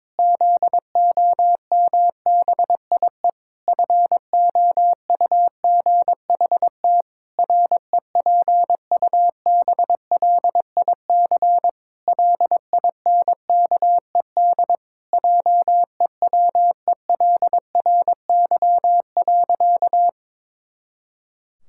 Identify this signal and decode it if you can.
Morse code, heard as a keyed tone, about 22 wpm, 700 Hz. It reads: ZOMBIE FOUGHT REPUBLIC LINKED JEWELRY.
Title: Pangram Challenge v2 - 22wpm